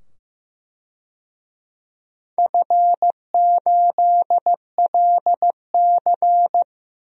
Morse code, heard as a keyed tone, about 15 wpm, 700 Hz. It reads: F8LC
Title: DXCC-100 Call Signs - 4 Characters - 15wpm